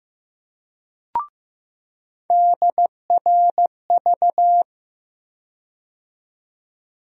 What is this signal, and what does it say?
Morse code, heard as a keyed tone, about 15 wpm, 700 Hz. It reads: DRV
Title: USA Call Sign Postfixes - 3 Letters 15wpm